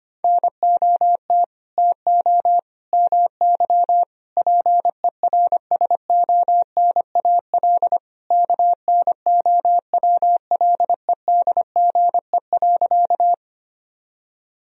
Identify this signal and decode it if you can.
Morse code, heard as a keyed tone, about 25 wpm, 700 Hz. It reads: NOT TO MY PERSONAL KNOWLEDGE.
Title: Sentences from Top 1000 Words - Review Part 4 - 25wpm